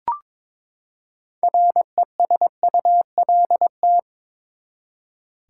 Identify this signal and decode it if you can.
Morse code, heard as a keyed tone, about 21 wpm, 700 Hz. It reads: RESULT
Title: Six Letter Words - 22wpm